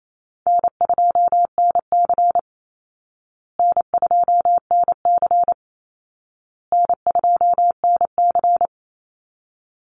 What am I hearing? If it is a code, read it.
Morse code, heard as a keyed tone, about 28 wpm, 700 Hz. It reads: N2NC N2NC N2NC
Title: Call Signs Repeated 2x - 28wpm